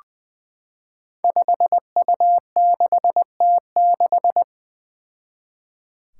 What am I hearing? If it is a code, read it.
Morse code, heard as a keyed tone, about 21 wpm, 700 Hz. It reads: 5U6T6
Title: Five Challenging Letter-Number Combinations 20wpm